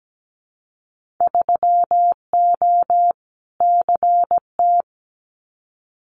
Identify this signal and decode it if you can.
Morse code, heard as a keyed tone, about 17 wpm, 700 Hz. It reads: 3O CT
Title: Winter Field Day Exchanges Encore 17wpm